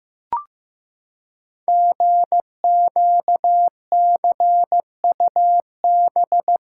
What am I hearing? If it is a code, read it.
Morse code, heard as a keyed tone, about 15 wpm, 700 Hz. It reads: GQCUB
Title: Code Groups - 5 Letter:Numbers - 15wpm